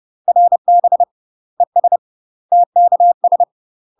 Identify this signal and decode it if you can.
Morse code, heard as a keyed tone, about 30 wpm, 700 Hz. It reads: RB ES TKS
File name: UK MASTER QSOs TRAINING / 30WPM / UK MASTER QSOs 0042